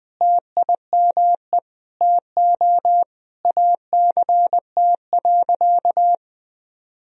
Morse code, heard as a keyed tone, about 20 wpm, 700 Hz. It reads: TIME TO ACT.
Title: Sentences from Top 1000 Words - Review Part 1 - 20wpm